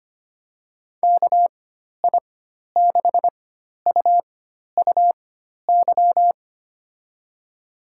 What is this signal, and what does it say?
Morse code, heard as a keyed tone, about 25 wpm, spaced as for 12 wpm, 700 Hz. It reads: KI6UUY
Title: USA Call Signs - 6 Characters - 4x Character Spacing - 25wpm